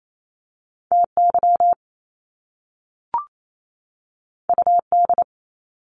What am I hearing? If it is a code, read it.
Morse code, heard as a keyed tone, about 28 wpm, 700 Hz. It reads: TY UD
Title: Top Two Letter Combinations in Words 28wpm